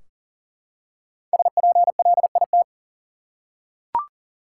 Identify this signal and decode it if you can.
Morse code, heard as a keyed tone, about 40 wpm, 700 Hz. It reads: SPLIT
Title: Five Letter Words - 40wpm